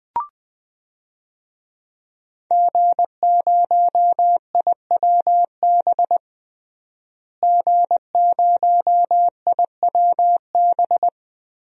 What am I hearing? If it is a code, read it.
Morse code, heard as a keyed tone, about 20 wpm, 700 Hz. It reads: G0IWB G0IWB
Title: DXCC-100 Call Signs - 5 Characters - Repeated 2x - 20wpm